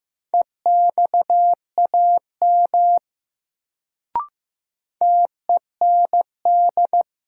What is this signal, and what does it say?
Morse code, heard as a keyed tone, about 15 wpm, 700 Hz. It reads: EXAM TEND
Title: Sentences from Top 1000 Words - Review Part 5 - New Word Drill - 15wpm